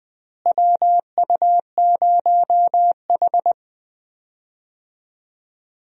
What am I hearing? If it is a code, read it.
Morse code, heard as a keyed tone, about 20 wpm, 700 Hz. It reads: WU0H
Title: USA Call Signs - 4 Characters - Repeated 2x - 20wpm